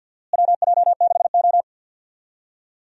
Morse code, heard as a keyed tone, about 45 wpm, 700 Hz. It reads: WJ6O
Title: USA Call Signs - 4 Characters - Repeated 2x - 50wpm